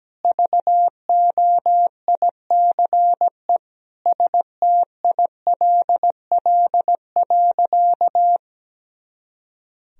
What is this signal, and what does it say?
Morse code, heard as a keyed tone, about 17 wpm, 700 Hz. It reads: VOICE STILL.
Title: Sentences from Top 500 Words and Abbreviations 17wpm